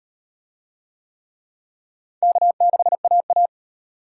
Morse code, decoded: K6AA

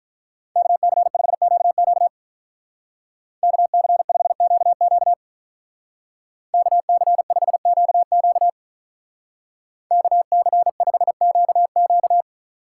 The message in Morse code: KC5QQ KC5QQ KC5QQ KC5QQ

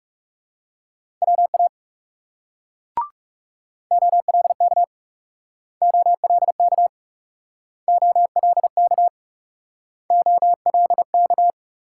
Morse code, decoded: WA OLK OLK OLK OLK

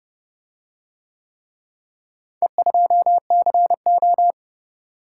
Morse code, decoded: E2CO

